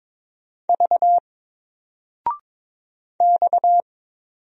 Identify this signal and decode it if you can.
Morse code, heard as a keyed tone, about 22 wpm, 700 Hz. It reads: V X
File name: Single-Letter-Number-22wpm